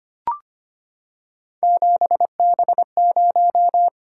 Morse code, heard as a keyed tone, about 25 wpm, 700 Hz. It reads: 7B0
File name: Three Letter-Number Combinations - 25wpm